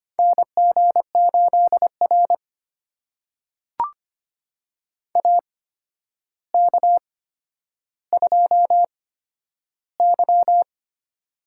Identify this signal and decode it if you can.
Morse code, heard as a keyed tone, about 25 wpm, spaced as for 11 wpm, 700 Hz. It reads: NG8R AK2Y
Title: USA Call Signs - 4 Characters - 8x Character Spacing - 25wpm